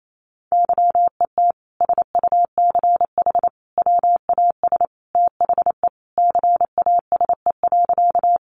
Morse code, heard as a keyed tone, about 28 wpm, 700 Hz. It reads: YET SUCH WAS THE CASE.
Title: Sentences from Top 700 Words 28wpm